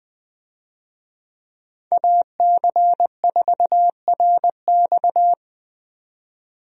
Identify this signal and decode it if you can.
Morse code, heard as a keyed tone, about 21 wpm, 700 Hz. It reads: AC4RX
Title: USA Call Signs - 5 Characters - 20wpm